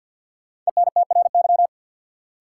Morse code, heard as a keyed tone, about 50 wpm, 700 Hz. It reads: ENTRY
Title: Sentences from Top 1500 Words - New Word Drill - 50wpm